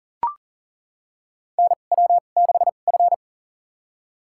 Morse code, heard as a keyed tone, about 38 wpm, spaced as for 28 wpm, 700 Hz. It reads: NW6F
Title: USA Call Signs - 4 Characters - 2x Character Spacing - 40wpm